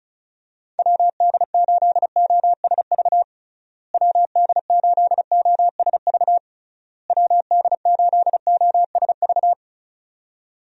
Morse code, heard as a keyed tone, about 36 wpm, 700 Hz. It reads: WD8OSV WD8OSV WD8OSV
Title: USA Call Signs - 6 Characters - Repeated 3x - 35wpm